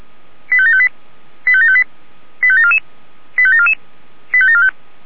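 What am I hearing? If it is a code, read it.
Five-tone selective call (ZVEI), heard as quick runs of five tones, tones of about 70 ms. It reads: 85758 85758 85744 85744 85745
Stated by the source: Received by SDR.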